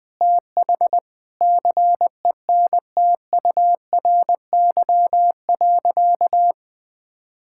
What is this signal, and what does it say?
Morse code, heard as a keyed tone, about 21 wpm, 700 Hz. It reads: TH CENTURY.
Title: Sentences with 6-Word Phrases - 20wpm